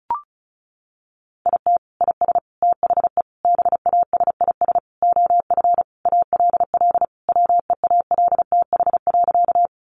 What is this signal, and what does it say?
Morse code, heard as a keyed tone, about 34 wpm, 700 Hz. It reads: IT IS THE BASIS OF ALL WEALTH.
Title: Sentences from Top 1200 Words 35wpm